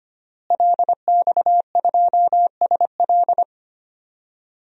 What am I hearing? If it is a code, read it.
Morse code, heard as a keyed tone, about 25 wpm, 700 Hz. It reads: LX2SL
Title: DXCC-100 Call Signs - 5 Characters - Repeated 3x - 25wpm